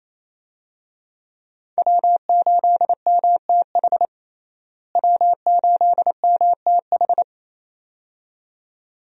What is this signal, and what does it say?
Morse code, heard as a keyed tone, about 27 wpm, 700 Hz. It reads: W8MTH W8MTH
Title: USA Call Signs - 5 Characters - Repeated 2x - 28wpm